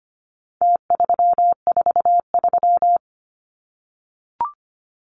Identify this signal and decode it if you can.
Morse code, heard as a keyed tone, about 25 wpm, 700 Hz. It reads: T343
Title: Numbers - Contest Style - 4-Digit 25wpm